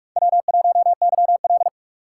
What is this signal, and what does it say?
Morse code, heard as a keyed tone, about 44 wpm, 700 Hz. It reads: W1YL
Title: USA Call Signs - 6x Character Spacing - 45wpm